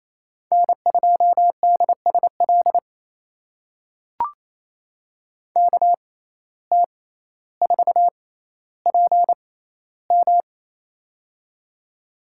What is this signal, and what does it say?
Morse code, heard as a keyed tone, about 28 wpm, spaced as for 14 wpm, 700 Hz. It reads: N2DSL KT4PM
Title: USA Call Signs - 5 Characters - 6x Character Spacing - 28wpm